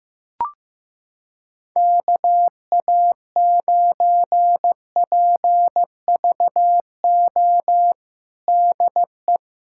Morse code, heard as a keyed tone, about 15 wpm, 700 Hz. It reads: KA9PVO DE
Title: USA Callsign DE Callsign - 6 Characters - 15wpm